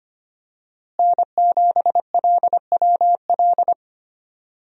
Morse code, heard as a keyed tone, about 25 wpm, 700 Hz. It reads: N7LWL
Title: USA Call Signs - 5 Characters - Repeated 3x - 25wpm